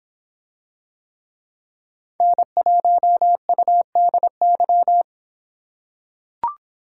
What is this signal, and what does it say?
Morse code, heard as a keyed tone, about 26 wpm, 700 Hz. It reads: N1UDY